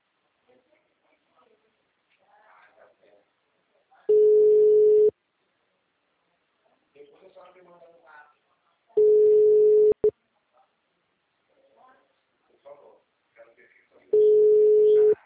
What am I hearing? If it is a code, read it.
Ringback tone (Europe, 425 Hz), with speech.